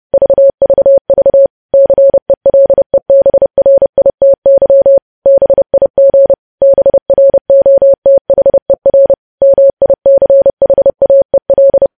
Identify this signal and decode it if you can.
Morse code, heard as a keyed tone, about 30 wpm, 550 Hz. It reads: VVV CELEBRITY BIG BROTHER MICHAEL
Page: news headlines in morse code at 30 wpm